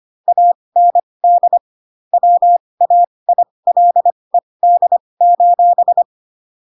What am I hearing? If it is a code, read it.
Morse code, heard as a keyed tone, about 25 wpm, spaced as for 20 wpm, 700 Hz. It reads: AND WAILED:
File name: Pinocchio / Pinocchio ch7 15 25 ik1hge